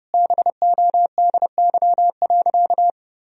Morse code, heard as a keyed tone, about 32 wpm, 700 Hz. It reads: BODY.
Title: Sentences from Top 1000 Words - Review Part 2 - 30wpm